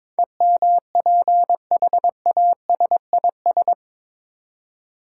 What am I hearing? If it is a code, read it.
Morse code, heard as a keyed tone, about 22 wpm, 700 Hz. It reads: EMPHASIS
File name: practice / 22wpm / Eight Letter Words - 22wpm